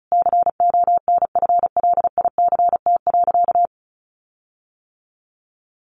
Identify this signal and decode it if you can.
Morse code, heard as a keyed tone, about 34 wpm, 700 Hz. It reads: CONFLICT.
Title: Sentences from Top 1300 Words 35wpm